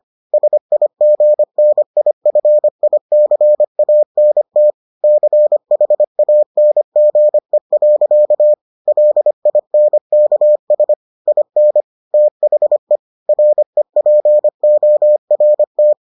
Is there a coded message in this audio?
Yes: Morse code from 0.3 to 15.9 s, heard as a keyed tone, about 25 wpm, 600 Hz. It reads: SIGNIFICANT CHANGE. LINKS IN THE REPORT